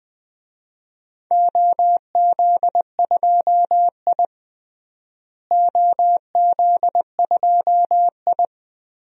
Morse code, heard as a keyed tone, about 20 wpm, 700 Hz. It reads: OZ2I OZ2I
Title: DXCC-100 Call Signs - 4 Characters - Repeated 2x - 20wpm